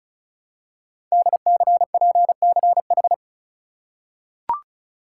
Morse code, heard as a keyed tone, about 36 wpm, 700 Hz. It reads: DCPCH
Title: Code Groups - 5 Letters - 35wpm